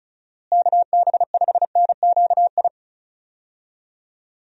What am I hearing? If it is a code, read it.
Morse code, heard as a keyed tone, about 36 wpm, 700 Hz. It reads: KB5NQI
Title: USA Call Signs - 6 Characters - Repeated 2x - 35wpm